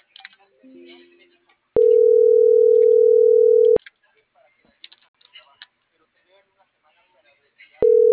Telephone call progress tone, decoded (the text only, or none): ringback tone